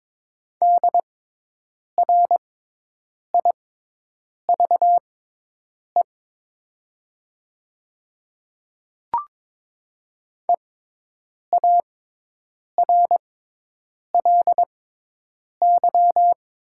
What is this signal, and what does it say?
Morse code, heard as a keyed tone, about 22 wpm, spaced as for 8 wpm, 700 Hz. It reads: DRIVE EARLY
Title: 6x Character Spacing - Top 500 Words - 22wpm